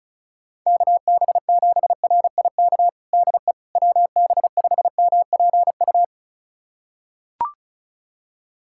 Morse code, heard as a keyed tone, about 36 wpm, 700 Hz. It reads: KB7RIK DE WB5MPU
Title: USA Callsign DE Callsign - 6 Characters - 35wpm